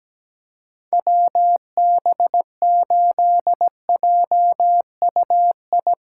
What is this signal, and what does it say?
Morse code, heard as a keyed tone, about 17 wpm, 700 Hz. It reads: WB8JUI
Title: K1USN SST QSO Practice - 17wpm